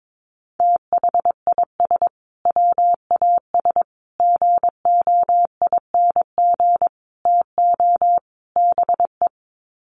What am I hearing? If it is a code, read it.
Morse code, heard as a keyed tone, about 22 wpm, 700 Hz. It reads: THIS WAS GOING TO BE
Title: Sets of 5 Words Encore2 22wpm